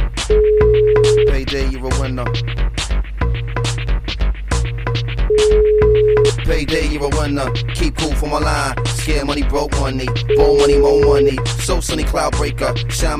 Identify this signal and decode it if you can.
Ringback tone (Europe, 425 Hz), with speech.